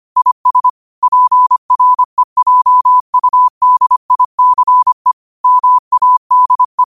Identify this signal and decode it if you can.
Morse code, heard as a keyed tone, about 25 wpm, 1000 Hz. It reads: IS PREJUDICE MADE
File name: qotdfiles / 25wpm